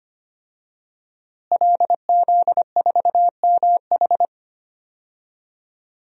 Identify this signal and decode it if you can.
Morse code, heard as a keyed tone, about 25 wpm, 700 Hz. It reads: LZ4MH
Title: DXCC-100 Call Signs - 5 Characters - Repeated 3x - 25wpm